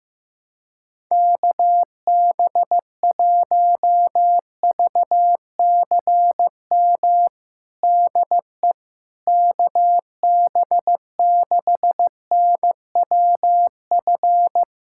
Morse code, heard as a keyed tone, about 16 wpm, 700 Hz. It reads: KB1VCM DE KB6NWF
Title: USA Callsign DE Callsign - 6 Characters - 15wpm